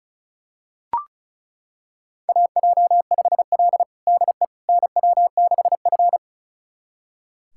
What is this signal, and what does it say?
Morse code, heard as a keyed tone, about 36 wpm, 700 Hz. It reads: AJ5L DE NW6F
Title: USA Callsign DE Callsign - 4 Characters - 35wpm